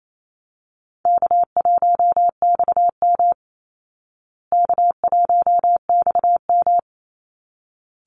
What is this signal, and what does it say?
Morse code, heard as a keyed tone, about 28 wpm, 700 Hz. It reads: K1XM K1XM
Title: Call Signs Repeated 3x - 28wpm